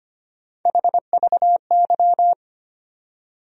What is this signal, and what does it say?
Morse code, heard as a keyed tone, about 25 wpm, 700 Hz. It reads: HVY
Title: USA Call Sign Postfixes - 3 Letters 25wpm